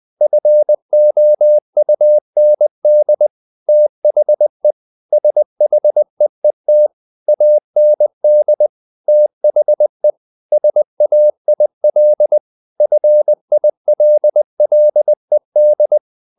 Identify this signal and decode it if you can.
Morse code, heard as a keyed tone, about 20 wpm, 600 Hz. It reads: FOUND THE SHEET AND THE SAIL FILLED
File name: the old man and the sea part 2 20wpm